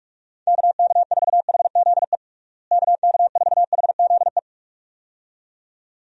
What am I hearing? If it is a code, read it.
Morse code, heard as a keyed tone, about 46 wpm, 700 Hz. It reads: KK4HZE KK4HZE
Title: USA Call Signs - 6 Characters - Repeated 2x - 45wpm